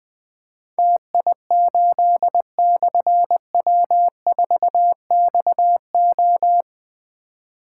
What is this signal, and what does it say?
Morse code, heard as a keyed tone, about 20 wpm, 700 Hz. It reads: TI8/W4XO
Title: Difficult Call Signs 20wpm